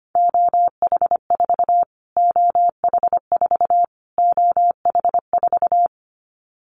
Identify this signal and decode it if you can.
Morse code, heard as a keyed tone, about 25 wpm, 700 Hz. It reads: OH4 OH4 OH4
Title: Callsign Prefixes for Top 25 DXCC - Repeated 3x - 25wpm